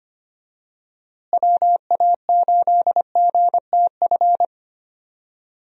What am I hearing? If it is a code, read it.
Morse code, heard as a keyed tone, about 25 wpm, 700 Hz. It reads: WA8GTF
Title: USA Call Signs - 6 Characters - Repeated 3x - 25wpm